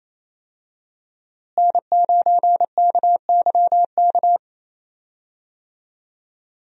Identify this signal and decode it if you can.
Morse code, heard as a keyed tone, about 28 wpm, 700 Hz. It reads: N9KYK
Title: USA Call Signs - 5 Characters - Repeated 2x - 28wpm